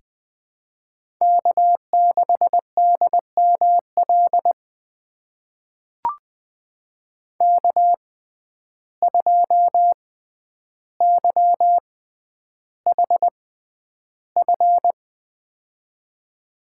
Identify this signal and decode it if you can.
Morse code, heard as a keyed tone, about 20 wpm, spaced as for 10 wpm, 700 Hz. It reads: K6DML K2YHF
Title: USA Call Signs - 5 Characters - 6x Character Spacing - 20wpm